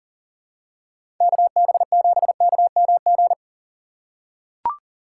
Morse code, heard as a keyed tone, about 40 wpm, 700 Hz. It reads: KB7KMG
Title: USA Call Signs - 6 Characters - 6x Character Spacing - 40wpm